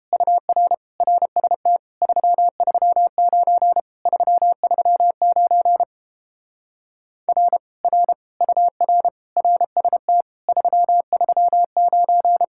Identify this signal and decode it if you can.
Morse code, heard as a keyed tone, about 31 wpm, 700 Hz. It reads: UR RST 339 339 R R UR RST 339